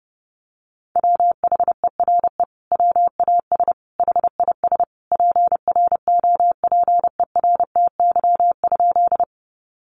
Morse code, heard as a keyed tone, about 30 wpm, 700 Hz. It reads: WHERE WAS HIS PROPERTY?
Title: Sentences from Top 1000 Words 30wpm